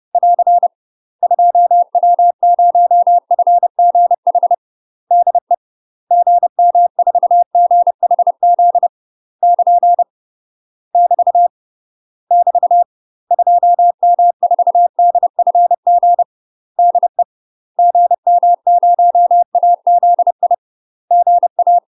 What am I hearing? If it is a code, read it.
Morse code, heard as a keyed tone, about 30 wpm, 700 Hz. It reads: <AR> 2W0FGH DE GM4GHZ <KN> <BT> <BT> 2M4DFG DE GM0AZI GA
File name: UK MASTER QSOs TRAINING / 30WPM / UK MASTER QSOs 0003